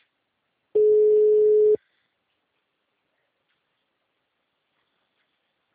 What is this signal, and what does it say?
Ringback tone (Europe, 425 Hz).
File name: out- -2002- - - .99